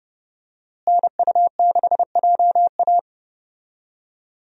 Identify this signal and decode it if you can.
Morse code, heard as a keyed tone, about 30 wpm, 700 Hz. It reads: NU6JA
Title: USA Call Signs - 5 Characters - Repeated 2x - 30wpm